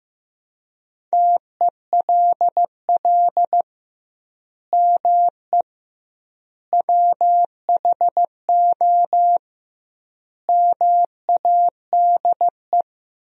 Morse code, heard as a keyed tone, about 15 wpm, 700 Hz. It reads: TELL ME WHO MADE
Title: 2x Word-Spacing - Sentences from Top 200 Words - 15wpm